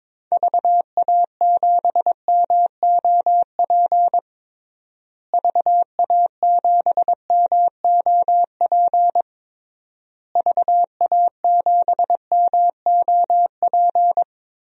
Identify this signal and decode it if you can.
Morse code, heard as a keyed tone, about 21 wpm, 700 Hz. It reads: VA7MOP VA7MOP VA7MOP
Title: DXCC-100 Call Signs - 6 Characters - Repeated 3x - 22wpm